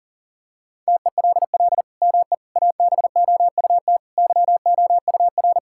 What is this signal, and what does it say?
Morse code, heard as a keyed tone, about 40 wpm, 700 Hz. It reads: TELL ME ABOUT YOUR
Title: Sentences from Top 1000 Words - Review Part 5 - 40wpm